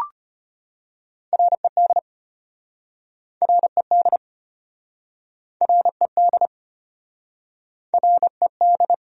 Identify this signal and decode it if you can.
Morse code, heard as a keyed tone, about 33 wpm, 700 Hz. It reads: RED RED RED RED